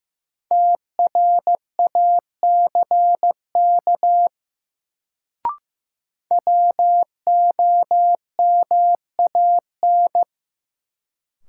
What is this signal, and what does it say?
Morse code, heard as a keyed tone, about 15 wpm, 700 Hz. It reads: TRACK WOMAN
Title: Sentences from Top 1100 Words - New Word Drill Alt - 15wpm